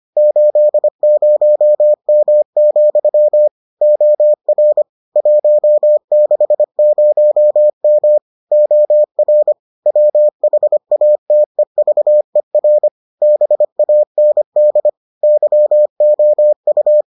Morse code, heard as a keyed tone, about 25 wpm, 600 Hz. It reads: 80M, OR 160M OR WHATEVER BAND YOU